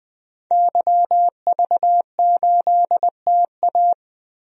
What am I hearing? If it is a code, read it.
Morse code, heard as a keyed tone, about 20 wpm, 700 Hz. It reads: YV8TA
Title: DXCC-100 Call Signs - 5 Characters - 20wpm